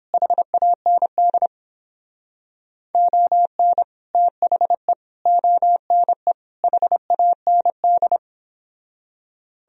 Morse code, heard as a keyed tone, about 31 wpm, 700 Hz. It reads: HAND ON THE ONE HAND